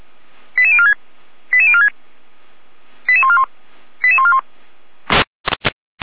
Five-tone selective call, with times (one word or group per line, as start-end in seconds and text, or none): 0.6-0.9 88046
1.5-1.9 88046
3.1-3.4 88141
4.0-4.4 88141